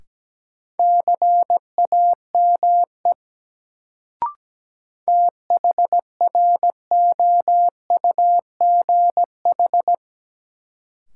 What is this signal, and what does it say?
Morse code, heard as a keyed tone, about 17 wpm, 700 Hz. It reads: CAME THROUGH
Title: Sentences from Top 200 Words - New Word Drill - 17wpm